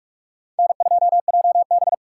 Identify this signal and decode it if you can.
Morse code, heard as a keyed tone, about 46 wpm, 700 Hz. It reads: N2JB
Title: USA Call Signs - 3x Character Spacing - 45wpm